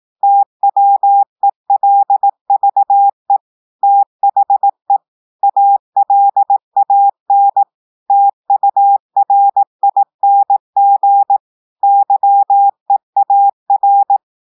Morse code, heard as a keyed tone, about 18 wpm, 800 Hz. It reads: TWELVE THE ALAN TURING YEAR